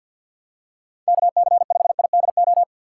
Morse code, heard as a keyed tone, about 53 wpm, 700 Hz. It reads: KC5IDO
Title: USA Call Signs - 6 Characters - 3x Character Spacing - 50wpm